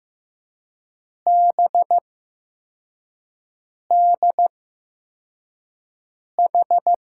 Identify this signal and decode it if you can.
Morse code, heard as a keyed tone, about 15 wpm, spaced as for 4 wpm, 700 Hz. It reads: BDH
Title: USA Call Signs - 8x Character Spacing - 15wpm